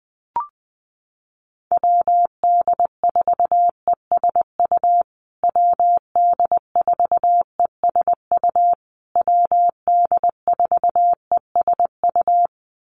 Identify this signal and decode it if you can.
Morse code, heard as a keyed tone, about 19 wpm, 700 Hz. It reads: WD4ESU WD4ESU WD4ESU
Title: USA Call Signs Repeated 3x 20wpm